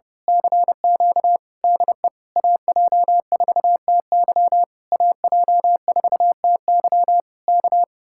Morse code, heard as a keyed tone, about 29 wpm, 700 Hz. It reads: CQ DE AJ4TY AJ4TY K
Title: Speed-Racing Common QSO Phrases - 30wpm